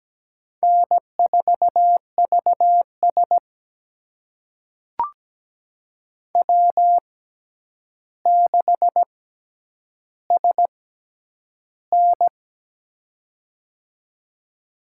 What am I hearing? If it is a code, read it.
Morse code, heard as a keyed tone, about 17 wpm, spaced as for 9 wpm, 700 Hz. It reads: N4VS W6SN
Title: USA Call Signs - 4 Characters - 6x Character Spacing - 17wpm